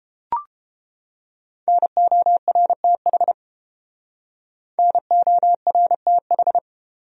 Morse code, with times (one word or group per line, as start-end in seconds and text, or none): 1.7-3.3 NORTH
4.8-6.6 NORTH